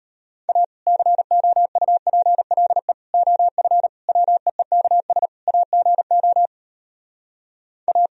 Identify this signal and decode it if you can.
Morse code, heard as a keyed tone, about 33 wpm, 700 Hz. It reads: A COUPLE OF WEEKS AGO A